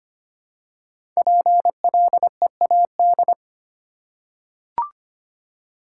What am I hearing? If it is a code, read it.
Morse code, heard as a keyed tone, about 25 wpm, 700 Hz. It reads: PLEAD
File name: 5 Letter Words - First 3 Letters Given - 25wpm